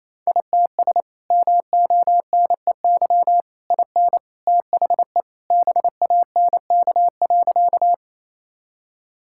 Morse code, heard as a keyed tone, about 27 wpm, 700 Hz. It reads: ITS MONEY IN THE BANK.
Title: Sentences from Top 1000 Words - Review Part 6 - 28wpm